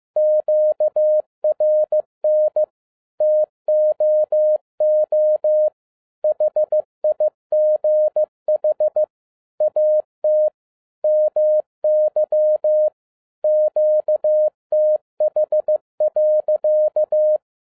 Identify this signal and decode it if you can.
Morse code, heard as a keyed tone, about 15 wpm, 610 Hz. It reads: QRN TOO HIGH AT MY QTH.